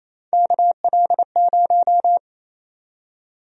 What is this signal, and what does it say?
Morse code, heard as a keyed tone, about 28 wpm, 700 Hz. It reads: KL0
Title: USA Call Sign Prefixes - 28wpm